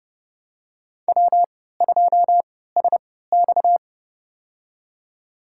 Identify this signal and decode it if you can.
Morse code, heard as a keyed tone, about 32 wpm, spaced as for 17 wpm, 700 Hz. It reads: W2SX
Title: USA Call Signs - 4 Characters - 3x Character Spacing - 30wpm